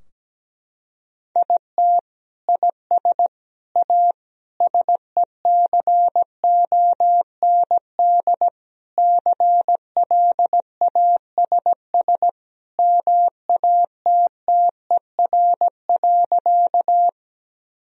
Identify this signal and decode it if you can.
Morse code, heard as a keyed tone, about 17 wpm, 700 Hz. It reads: IT IS A SECOND CLASS MATTER.
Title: Sentences from Top 1000 Words - Review Part 2 - 17wpm